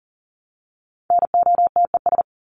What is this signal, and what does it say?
Morse code, heard as a keyed tone, about 40 wpm, 700 Hz. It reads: NOTES
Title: ES:S Suffix - Top 100 Words with Base Words 40wpm